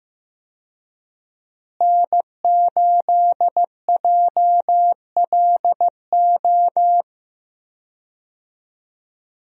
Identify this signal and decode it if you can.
Morse code, heard as a keyed tone, about 15 wpm, 700 Hz. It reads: N8JLO
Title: USA Call Signs - 5 Characters - Repeated 3x - 15wpm